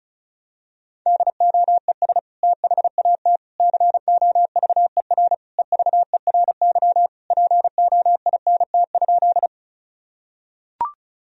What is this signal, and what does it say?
Morse code, heard as a keyed tone, about 36 wpm, 700 Hz. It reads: DOES THAT COVER EVERY POINT?
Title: Sentences from Top 1000 Words - Review Part 1 - 35wpm